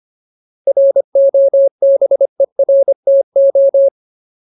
Morse code, heard as a keyed tone, about 25 wpm, 550 Hz. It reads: ROBERTO
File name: Ultimate Word List / 25WPM Ultimate